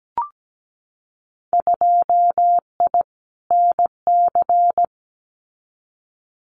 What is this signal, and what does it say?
Morse code, heard as a keyed tone, about 17 wpm, 700 Hz. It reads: 2I NC
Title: Winter Field Day Exchanges 17wpm